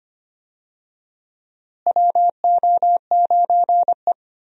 Morse code, heard as a keyed tone, about 25 wpm, 700 Hz. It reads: WO9E
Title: USA Call Signs - 4 Characters - Repeated 2x - 25wpm